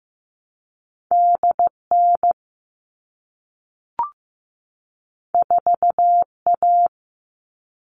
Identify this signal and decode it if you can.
Morse code, heard as a keyed tone, about 15 wpm, 700 Hz. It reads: DN 4A
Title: All Cut Numbers - 2-Digit 15wpm